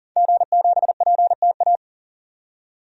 Morse code, heard as a keyed tone, about 37 wpm, 700 Hz. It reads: G7PTA
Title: DXCC-100 Call Signs - 5 Characters - 40wpm